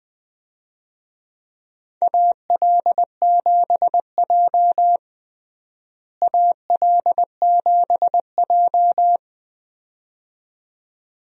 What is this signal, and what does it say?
Morse code, heard as a keyed tone, about 21 wpm, 700 Hz. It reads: AL7J AL7J
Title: USA Call Signs - 4 Characters - Repeated 2x - 20wpm